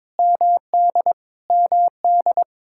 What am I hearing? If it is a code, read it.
Morse code, heard as a keyed tone, about 22 wpm, 700 Hz. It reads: MD MD
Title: Practice QSO - Exchange 1 - Beginning - 22wpm